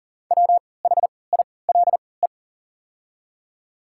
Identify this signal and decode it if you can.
Morse code, heard as a keyed tone, about 40 wpm, spaced as for 23 wpm, 700 Hz. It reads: WHILE
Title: 3x Character Spacing - Top 300 Words - 40wpm